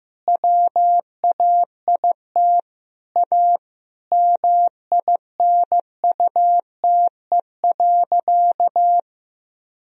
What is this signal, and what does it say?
Morse code, heard as a keyed tone, about 15 wpm, 700 Hz. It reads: WAIT A MINUTE.
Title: Sentences from Top 500 Words and Abbreviations 15wpm